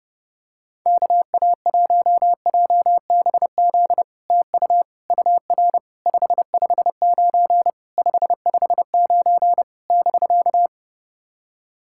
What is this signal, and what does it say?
Morse code, heard as a keyed tone, about 30 wpm, 700 Hz. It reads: KA1JBZ TU UR 559 559 <BK>
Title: SOTA QSO Practice 30wpm